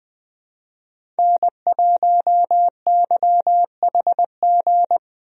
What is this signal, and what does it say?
Morse code, heard as a keyed tone, about 20 wpm, 700 Hz. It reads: N1YHG
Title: USA Call Signs - 5 Characters - 20wpm